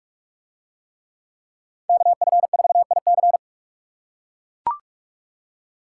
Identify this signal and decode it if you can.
Morse code, heard as a keyed tone, about 44 wpm, 700 Hz. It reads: KF4IC